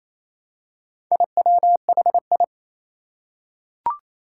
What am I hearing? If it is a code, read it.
Morse code, heard as a keyed tone, about 29 wpm, 700 Hz. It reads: IWHI
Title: Four Challenging Letter-Number Combinations 28wpm